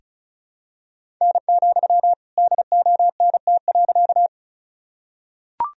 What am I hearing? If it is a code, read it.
Morse code, heard as a keyed tone, about 36 wpm, 700 Hz. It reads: N, DONT.
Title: Sentences from Top 300 Words and Abbreviations 35wpm